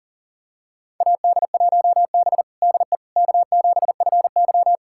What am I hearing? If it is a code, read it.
Morse code, heard as a keyed tone, about 37 wpm, 700 Hz. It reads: AD1B DE K7FY
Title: USA Callsign DE Callsign - 4 Characters - 40wpm